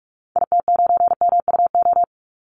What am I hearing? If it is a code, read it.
Morse code, heard as a keyed tone, about 44 wpm, 700 Hz. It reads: IT9MUO
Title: Call Signs - 45wpm